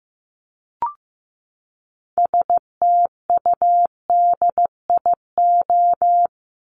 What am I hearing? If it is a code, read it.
Morse code, heard as a keyed tone, about 15 wpm, 700 Hz. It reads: STUDIO
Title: Top 1001-1250 Words 15wpm